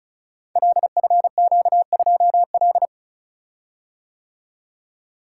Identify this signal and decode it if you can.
Morse code, heard as a keyed tone, about 36 wpm, 700 Hz. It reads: LFQ2L
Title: Code Groups - 5 Letter:Numbers - 35wpm